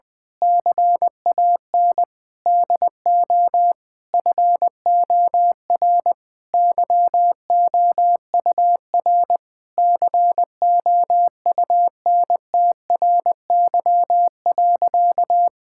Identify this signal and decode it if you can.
Morse code, heard as a keyed tone, about 20 wpm, 700 Hz. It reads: CAN DO FOR YOUR COUNTRY.
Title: Sentences with 9-Word Phrases - 20wpm